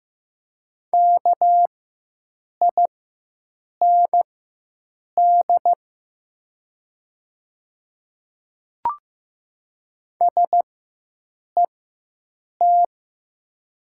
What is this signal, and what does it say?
Morse code, heard as a keyed tone, about 15 wpm, spaced as for 7 wpm, 700 Hz. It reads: KIND SET
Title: 4x Character Spacing - Top 300 Words - 15wpm